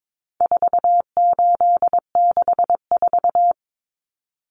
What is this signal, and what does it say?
Morse code, heard as a keyed tone, about 22 wpm, 700 Hz. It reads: 4864
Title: Numbers - 4-Digit 22wpm